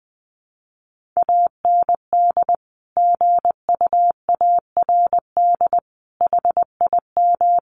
Morse code, heard as a keyed tone, about 20 wpm, 700 Hz. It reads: AND GUARD HIM
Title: Sentences from Top 1400 Words 20wpm